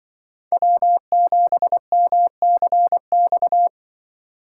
Morse code, heard as a keyed tone, about 24 wpm, 700 Hz. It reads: W7MCX
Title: Speed Racing - USA 1x3 Call Signs 20wpm